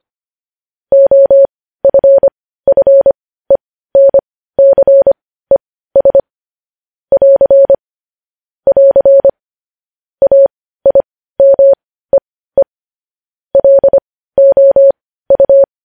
Morse code, heard as a keyed tone, about 25 wpm, spaced as for 15 wpm, 550 Hz. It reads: OFFENCES <AR> <AR> AIMEE LOU
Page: News Headlines in Morse Code at 15 WPM